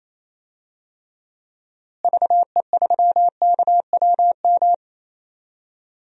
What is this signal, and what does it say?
Morse code, heard as a keyed tone, about 27 wpm, 700 Hz. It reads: VE3KWM
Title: DXCC-100 Call Signs - 6 Characters - Repeated 3x - 28wpm